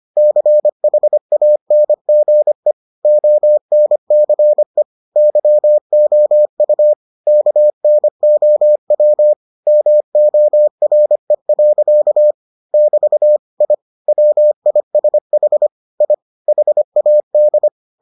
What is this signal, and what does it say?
Morse code, heard as a keyed tone, about 25 wpm, 600 Hz. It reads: CHANGE ONCE YOU KNOW MORE. <BT> I WISH I HAD